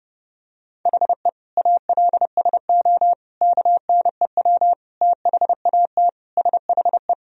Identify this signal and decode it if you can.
Morse code, heard as a keyed tone, about 32 wpm, 700 Hz. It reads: HE ALSO KNEW THAT SHE
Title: Sentences from Top 1000 Words - Review Part 1 - 30wpm